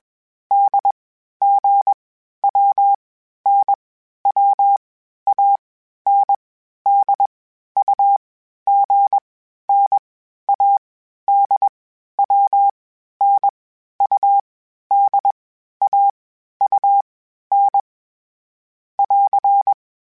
Morse code, heard as a keyed tone, about 21 wpm, 800 Hz. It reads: D G W N W A N D U G N A D W N U D A U N <AR>
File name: Groups@20 / Groups@20 mp3s / AWUNDG@20